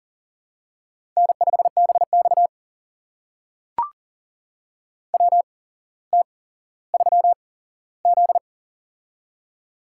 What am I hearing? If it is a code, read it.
Morse code, heard as a keyed tone, about 38 wpm, spaced as for 17 wpm, 700 Hz. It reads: N5BX WT3Z
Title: USA Call Signs - 4 Characters - 8x Character Spacing - 40wpm